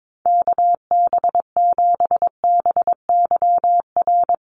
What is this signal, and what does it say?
Morse code, heard as a keyed tone, about 22 wpm, 700 Hz. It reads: KB7BYR
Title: USA Call Signs Repeated 2x 22wpm